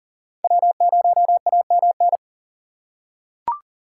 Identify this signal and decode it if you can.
Morse code, heard as a keyed tone, about 39 wpm, 700 Hz. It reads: W0AMN
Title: USA Call Signs - 5 Characters - 4x Character Spacing - 40wpm